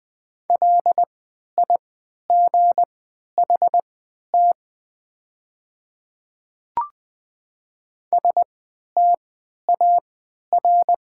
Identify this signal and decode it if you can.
Morse code, heard as a keyed tone, about 20 wpm, spaced as for 11 wpm, 690 Hz. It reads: LIGHT STAR
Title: 3x Character Spacing - Top 500 Words - 20wpm